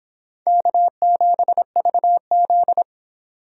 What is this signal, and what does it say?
Morse code, heard as a keyed tone, about 25 wpm, 700 Hz. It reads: K7VZ